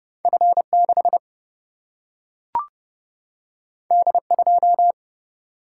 Morse code, heard as a keyed tone, about 30 wpm, 700 Hz. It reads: F 6 D 2
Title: Single-Letter Single-Number 30wpm